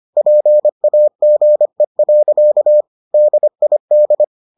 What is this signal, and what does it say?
Morse code, heard as a keyed tone, about 25 wpm, 600 Hz. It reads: PAGE. DID